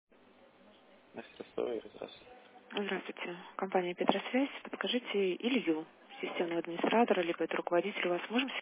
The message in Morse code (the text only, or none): none